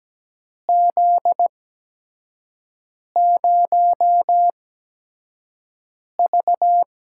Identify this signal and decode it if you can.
Morse code, heard as a keyed tone, about 17 wpm, spaced as for 5 wpm, 700 Hz. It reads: Z0V
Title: USA Call Signs - 4 Characters - 8x Character Spacing - 17wpm